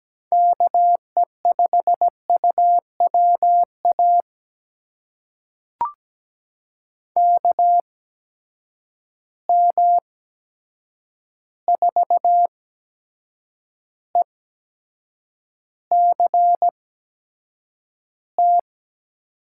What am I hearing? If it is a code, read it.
Morse code, heard as a keyed tone, about 17 wpm, spaced as for 7 wpm, 700 Hz. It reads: KE5UWA KM4ECT
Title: USA Call Signs - 6 Characters - 8x Character Spacing - 17wpm